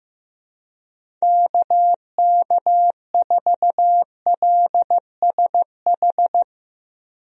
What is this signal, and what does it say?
Morse code, heard as a keyed tone, about 15 wpm, 700 Hz. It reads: KK4LSH
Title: USA Call Signs - 6 Characters - Repeated 3x - 15wpm